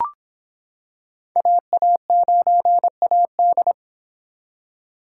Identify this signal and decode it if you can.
Morse code, heard as a keyed tone, about 25 wpm, 700 Hz. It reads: AA9AD